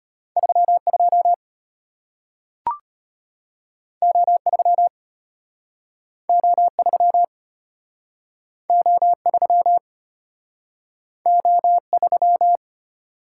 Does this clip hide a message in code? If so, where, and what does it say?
Yes: Morse code from 0.4 to 12.6 s, heard as a keyed tone, about 34 wpm, 700 Hz. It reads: 32 O3 O3 O3 O3